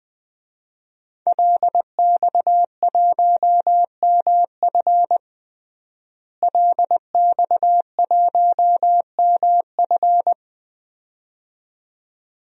Morse code, heard as a keyed tone, about 21 wpm, 700 Hz. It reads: LX1MF LX1MF
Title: DXCC-100 Call Signs - 5 Characters - Repeated 2x - 20wpm